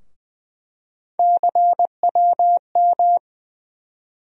Morse code, heard as a keyed tone, about 20 wpm, 700 Hz. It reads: CWM
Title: USA Call Sign Postfixes - 3 Letters 20wpm